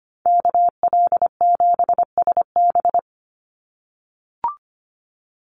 Morse code, heard as a keyed tone, about 25 wpm, 700 Hz. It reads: KL7SB
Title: Call Signs - 25wpm